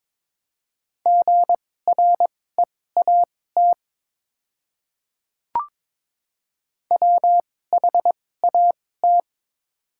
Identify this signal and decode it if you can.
Morse code, heard as a keyed tone, about 22 wpm, spaced as for 16 wpm, 700 Hz. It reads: GREAT WHAT
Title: 2x Character Spacing - Top 200 Words - 22wpm